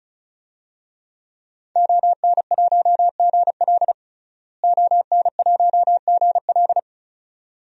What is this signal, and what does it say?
Morse code, heard as a keyed tone, about 35 wpm, 700 Hz. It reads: ON1GL ON1GL
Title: DXCC-100 Call Signs - 5 Characters - Repeated 2x - 35wpm